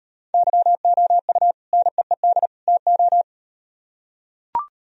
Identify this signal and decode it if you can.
Morse code, heard as a keyed tone, about 38 wpm, 700 Hz. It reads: YOU NEED TO